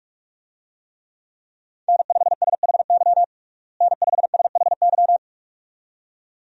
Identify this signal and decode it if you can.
Morse code, heard as a keyed tone, about 46 wpm, 700 Hz. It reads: N5SHY N5SHY
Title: USA Call Signs - 5 Characters - Repeated 2x - 45wpm